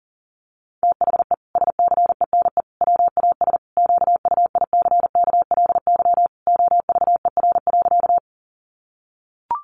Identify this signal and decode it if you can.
Morse code, heard as a keyed tone, about 40 wpm, 700 Hz. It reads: THE SCENE WAS QUICKLY OVER.
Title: Sentences from Top 1100 Words 40wpm